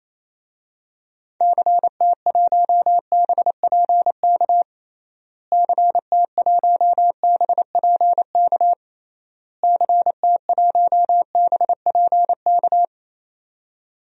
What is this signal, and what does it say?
Morse code, heard as a keyed tone, about 28 wpm, 700 Hz. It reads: CT1BPK CT1BPK CT1BPK
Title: DXCC-100 Call Signs - 6 Characters - Repeated 3x - 28wpm